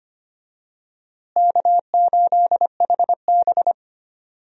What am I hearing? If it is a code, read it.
Morse code, heard as a keyed tone, about 25 wpm, 700 Hz. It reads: K8HB
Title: USA Call Signs - 4 Characters - Repeated 2x - 25wpm